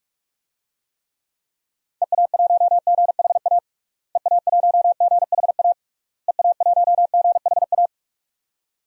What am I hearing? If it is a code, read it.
Morse code, heard as a keyed tone, about 46 wpm, 700 Hz. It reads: EA1GHA EA1GHA EA1GHA
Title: DXCC-100 Call Signs - 6 Characters - Repeated 3x - 45wpm